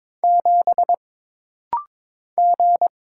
Morse code, heard as a keyed tone, about 22 wpm, 700 Hz. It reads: 7 G
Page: This rapid-fire timing is geared to push your ICR (Instant Character Recognition) skills.